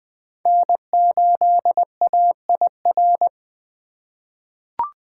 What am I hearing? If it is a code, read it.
Morse code, heard as a keyed tone, about 20 wpm, 700 Hz. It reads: N8AIR
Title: USA Call Signs - 5 Characters - 3x Character Spacing - 20wpm